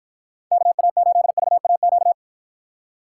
Morse code, heard as a keyed tone, about 53 wpm, 700 Hz. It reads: KA8VAQ